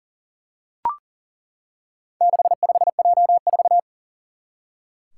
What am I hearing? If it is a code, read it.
Morse code, heard as a keyed tone, about 40 wpm, 700 Hz. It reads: 65J4
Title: Four Challenging Letter-Number Combinations 40wpm